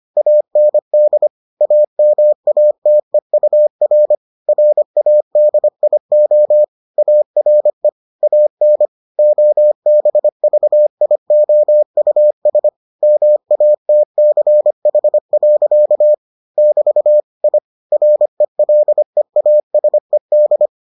Morse code, heard as a keyed tone, about 25 wpm, 600 Hz. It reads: AND AMATEUR RADIO ARE AN OBVIOUS MATCH. <BT> I RELEASED